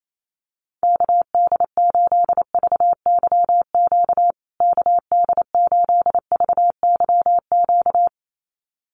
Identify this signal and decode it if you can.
Morse code, heard as a keyed tone, about 28 wpm, 700 Hz. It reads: KD8VYQ KD8VYQ
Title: USA Call Signs Repeated 2x 28wpm